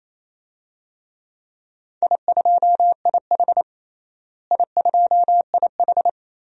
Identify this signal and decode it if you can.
Morse code, heard as a keyed tone, about 28 wpm, 700 Hz. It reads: I2IH I2IH
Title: DXCC-100 Call Signs - 4 Characters - Repeated 2x - 28wpm